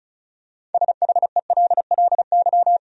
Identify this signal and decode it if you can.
Morse code, heard as a keyed tone, about 34 wpm, 700 Hz. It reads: SHELLY
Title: USA Common Names - 35wpm